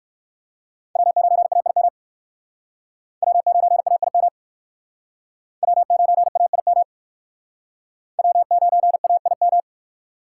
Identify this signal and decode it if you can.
Morse code, heard as a keyed tone, about 58 wpm, 700 Hz. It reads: W9AIM W9AIM W9AIM W9AIM